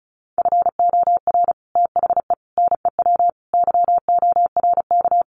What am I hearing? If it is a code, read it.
Morse code, heard as a keyed tone, about 34 wpm, 700 Hz. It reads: FOR THE NEW YORK
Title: Sets of 5 Words 35wpm